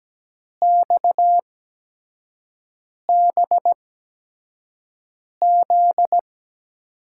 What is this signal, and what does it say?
Morse code, heard as a keyed tone, about 17 wpm, spaced as for 5 wpm, 700 Hz. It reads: XBZ
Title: USA Call Signs - 5 Characters - 8x Character Spacing - 17wpm